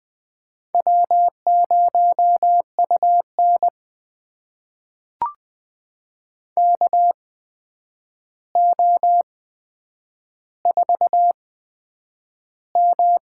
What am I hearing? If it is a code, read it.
Morse code, heard as a keyed tone, about 20 wpm, spaced as for 9 wpm, 700 Hz. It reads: W0UN KO4M
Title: USA Call Signs - 4 Characters - 8x Character Spacing - 20wpm